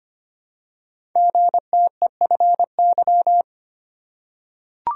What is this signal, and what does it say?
Morse code, heard as a keyed tone, about 25 wpm, 700 Hz. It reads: GTEFY
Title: Code Groups - 5 Letter:Numbers - 25wpm